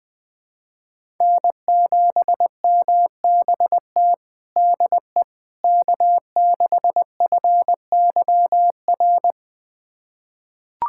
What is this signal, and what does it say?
Morse code, heard as a keyed tone, about 21 wpm, 700 Hz. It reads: N7MBT DE K6FYR
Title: USA Callsign DE Callsign - 5 Characters - 20wpm